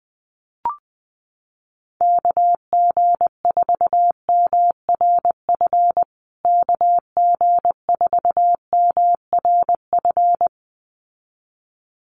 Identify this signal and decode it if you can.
Morse code, heard as a keyed tone, about 19 wpm, 700 Hz. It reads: KG4MRF KG4MRF
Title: USA Call Signs Repeated 2x 20wpm